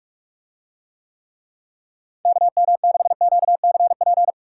Morse code, heard as a keyed tone, about 44 wpm, 700 Hz. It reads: KM6QCP